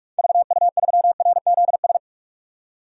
Morse code, heard as a keyed tone, about 46 wpm, 700 Hz. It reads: VU3RZS
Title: DXCC-100 Call Signs - 6 Characters - 45wpm